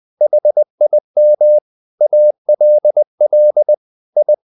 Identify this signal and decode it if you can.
Morse code, heard as a keyed tone, about 20 wpm, 600 Hz. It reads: HIM ALL I
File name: the old man and the sea / the old man and the sea part 1 20wpm